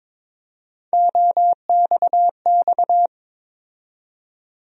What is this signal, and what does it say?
Morse code, heard as a keyed tone, about 22 wpm, 700 Hz. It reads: OXX
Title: USA Call Sign Postfixes - 3 Letters 22wpm